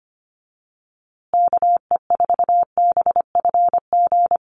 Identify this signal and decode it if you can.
Morse code, heard as a keyed tone, about 25 wpm, 700 Hz. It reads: KE4BFG
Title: USA Call Signs Repeated 3x 25wpm